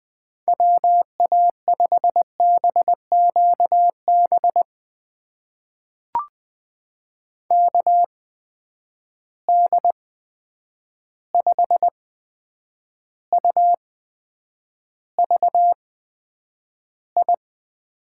Morse code, heard as a keyed tone, about 20 wpm, spaced as for 9 wpm, 700 Hz. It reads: WA5BQB KD5UVI
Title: USA Call Signs - 6 Characters - 8x Character Spacing - 20wpm